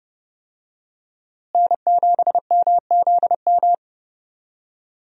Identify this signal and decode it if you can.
Morse code, heard as a keyed tone, about 30 wpm, 700 Hz. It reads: N7MZM
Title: USA Call Signs - 5 Characters - Repeated 3x - 30wpm